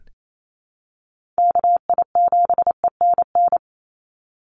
Morse code, heard as a keyed tone, about 28 wpm, 700 Hz. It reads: KI7ENN
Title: USA Call Signs 28wpm